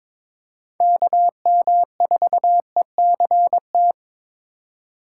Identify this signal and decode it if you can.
Morse code, heard as a keyed tone, about 22 wpm, 700 Hz. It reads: KM4ECT
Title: USA Call Signs - 6 Characters - 4x Character Spacing - 22wpm